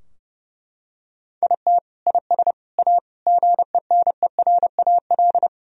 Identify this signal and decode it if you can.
Morse code, heard as a keyed tone, about 30 wpm, 700 Hz. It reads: IT IS A GENERAL
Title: Sentences from Top 1000 Words - Review Part 3 - 30wpm